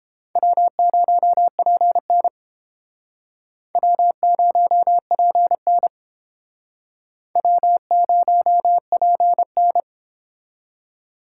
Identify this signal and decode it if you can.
Morse code, heard as a keyed tone, about 30 wpm, 700 Hz. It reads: W0PN W0PN W0PN